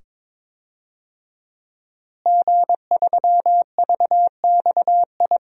Morse code, heard as a keyed tone, about 22 wpm, 700 Hz. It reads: G3VXI
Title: DXCC-100 Call Signs - 5 Characters - Repeated 2x - 22wpm